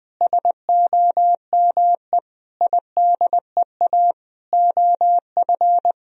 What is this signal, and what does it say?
Morse code, heard as a keyed tone, about 20 wpm, 700 Hz. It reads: SOME IDEA OF
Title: Sentences from Top 1000 Words - Review Part 3 - 20wpm